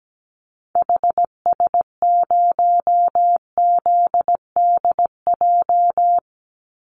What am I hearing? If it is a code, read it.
Morse code, heard as a keyed tone, about 17 wpm, 700 Hz. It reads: HS0ZDJ
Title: Call Signs Repeated 3x - 17wpm